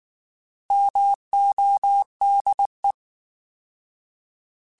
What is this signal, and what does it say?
Morse code, heard as a keyed tone, about 19 wpm, 800 Hz. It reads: MODE